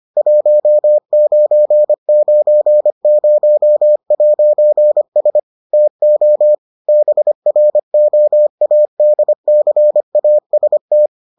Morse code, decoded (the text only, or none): 1990'S TO BROADCAST